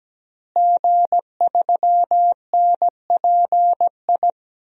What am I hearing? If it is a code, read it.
Morse code, heard as a keyed tone, about 17 wpm, 700 Hz. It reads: G3NPI
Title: DXCC-100 Call Signs - 5 Characters - Repeated 2x - 17wpm